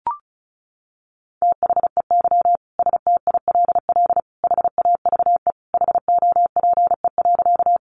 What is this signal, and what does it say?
Morse code, heard as a keyed tone, about 34 wpm, 700 Hz. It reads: THEY STILL HAVE HOPE.
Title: Sentences from Top 1000 Words 35wpm